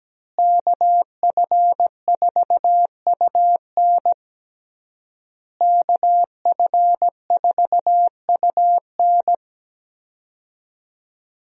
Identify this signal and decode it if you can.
Morse code, heard as a keyed tone, about 17 wpm, 700 Hz. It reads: KF4UN KF4UN
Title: USA Call Signs - 5 Characters - Repeated 3x - 17wpm